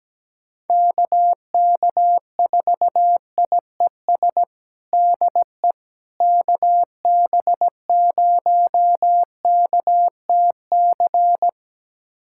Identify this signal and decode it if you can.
Morse code, heard as a keyed tone, about 17 wpm, 700 Hz. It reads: KK4IES DE KB0KTC
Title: USA Callsign DE Callsign 17wpm